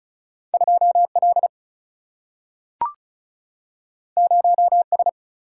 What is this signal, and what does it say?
Morse code, heard as a keyed tone, about 36 wpm, 700 Hz. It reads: 2L 0S
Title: Single-Letter Single-Number 35wpm